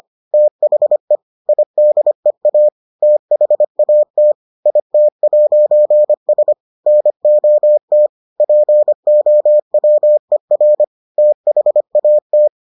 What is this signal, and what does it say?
Morse code, heard as a keyed tone, about 25 wpm, 600 Hz. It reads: THE IDEA THAT IT'S NOT POWER THAT